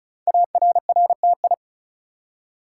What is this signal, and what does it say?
Morse code, heard as a keyed tone, about 36 wpm, 690 Hz. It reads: ARRTI
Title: Code Groups - 5 Letters - 35wpm